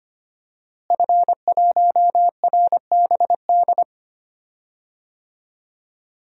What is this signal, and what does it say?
Morse code, heard as a keyed tone, about 25 wpm, 700 Hz. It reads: F1RBD
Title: DXCC-100 Call Signs - 5 Characters - Repeated 2x - 25wpm